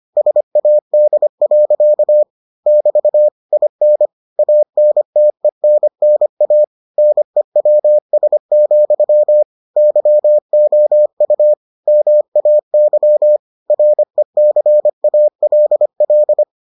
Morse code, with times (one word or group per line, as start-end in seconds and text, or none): 0.2-2.2 SAD.
2.7-3.3 <BT>
3.5-4.1 IN
4.4-6.6 ANTENNA
7.0-9.4 NEWS,
9.8-11.5 YOU
11.9-13.4 MAY
13.7-16.4 RECALL